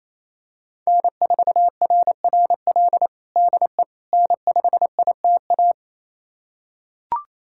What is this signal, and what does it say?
Morse code, heard as a keyed tone, about 28 wpm, 700 Hz. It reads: N4RRL DE N5ITA
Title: USA Callsign DE Callsign - 5 Characters - 28wpm